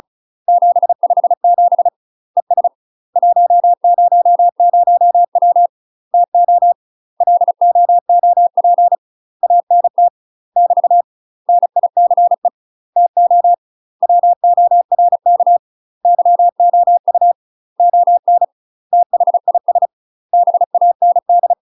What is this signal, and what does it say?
Morse code, heard as a keyed tone, about 34 wpm, 700 Hz. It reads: 757 ES 100W TO LOOP ANT <BT> NICE TO WORK YOU ON THIS BAND
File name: UK MASTER QSOs TRAINING / 35WPM / UK MASTER QSOs 0015